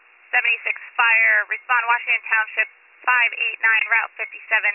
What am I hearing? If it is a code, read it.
Emergency Alert System (SAME) tones, heard as harsh buzzing bursts, with speech. It is heard from 1.1 to 4.0 s.